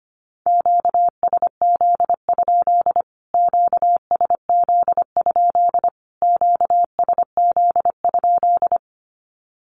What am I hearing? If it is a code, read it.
Morse code, heard as a keyed tone, about 25 wpm, 700 Hz. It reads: QSZ? QSZ? QSZ?
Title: Q-Codes Repeated 3x 25wpm